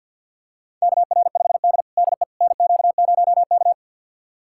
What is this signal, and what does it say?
Morse code, heard as a keyed tone, about 51 wpm, 700 Hz. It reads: KR5D DE NQ0K
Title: USA Callsign DE Callsign - 4 Characters - 50wpm